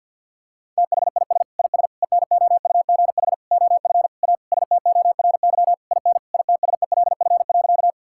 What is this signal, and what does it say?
Morse code, heard as a keyed tone, about 50 wpm, 700 Hz. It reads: THIS IS ENOUGH OF A STORY IN ITSELF.
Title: Sentences from Top 1000 Words - Review Part 5 - 50wpm